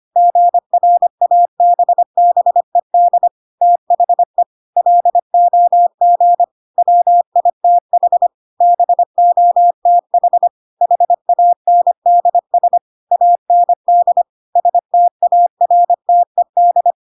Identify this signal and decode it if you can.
Morse code, heard as a keyed tone, about 25 wpm, 700 Hz. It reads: GRABBED THE LOG WITH BOTH HANDS AND STARTED
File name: Pinocchio / Pinocchio ch1 25 25 ik1hge